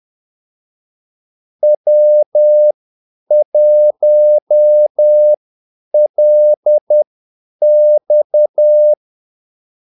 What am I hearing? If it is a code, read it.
Morse code, heard as a keyed tone, about 10 wpm, spaced as for 8 wpm, 600 Hz. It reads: W1LX
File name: Callsigns-10-8wpm